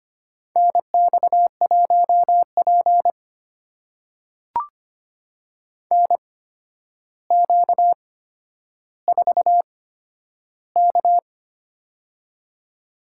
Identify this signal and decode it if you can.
Morse code, heard as a keyed tone, about 25 wpm, spaced as for 11 wpm, 700 Hz. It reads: NX1P NQ4K
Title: USA Call Signs - 4 Characters - 8x Character Spacing - 25wpm